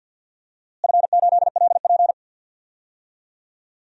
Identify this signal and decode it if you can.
Morse code, heard as a keyed tone, about 50 wpm, 700 Hz. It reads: F8LP
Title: DXCC-100 Call Signs - 4 Characters - Repeated 3x - 50wpm